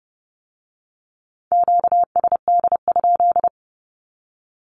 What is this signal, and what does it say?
Morse code, heard as a keyed tone, about 30 wpm, 700 Hz. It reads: QSD?
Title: Q-Codes with Spoken Meaning - 30wpm